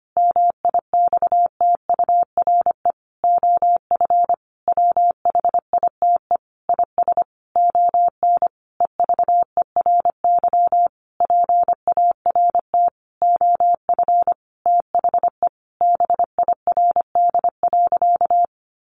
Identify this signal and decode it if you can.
Morse code, heard as a keyed tone, about 25 wpm, 700 Hz. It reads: MIXTURE OF WHITE IS ON EVERY PART OF THE BIRD.
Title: Sentences from Top 1500 Words 25wpm